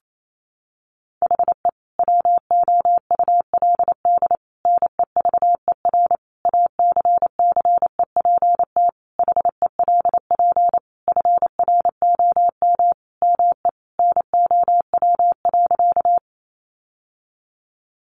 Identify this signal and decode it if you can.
Morse code, heard as a keyed tone, about 28 wpm, 700 Hz. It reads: HE WOULD NEVER ACCEPT HELP FROM ME NOW.
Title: Sentences from Top 1300 Words 28wpm